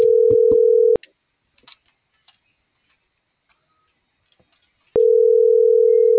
Ringback tone (North America), 440 + 480 Hz.